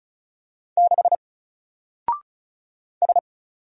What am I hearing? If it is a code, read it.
Morse code, heard as a keyed tone, about 36 wpm, 700 Hz. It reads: 6 S
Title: Single Letter-Number 35wpm